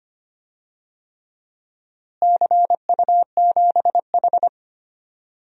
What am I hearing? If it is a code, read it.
Morse code, heard as a keyed tone, about 25 wpm, 700 Hz. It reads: CU7H
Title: DXCC-100 Call Signs - 4 Characters - Repeated 3x - 25wpm